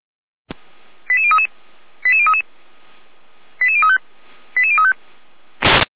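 Five-tone selective call (ZVEI), heard as quick runs of five tones, tones of about 70 ms. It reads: 88033 88033 88035 88035
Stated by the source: ZVEI five-tone selective call received by SDR.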